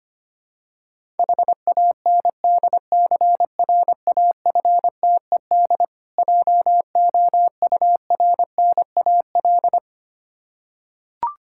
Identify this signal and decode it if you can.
Morse code, heard as a keyed tone, about 25 wpm, 700 Hz. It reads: HANDCRAFTED JOURNAL
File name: practice / 25wpm / Difficult Words with Hints - 25wpm